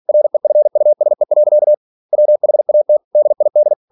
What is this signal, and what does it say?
Morse code, heard as a keyed tone, about 46 wpm, 600 Hz. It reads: REFUSE. WHAT DID
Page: Never 47wpm